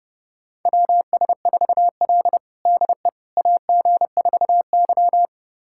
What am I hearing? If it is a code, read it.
Morse code, heard as a keyed tone, about 30 wpm, 700 Hz. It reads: WS4L DE AG4Y
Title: USA Callsign DE Callsign - 4 Characters - 30wpm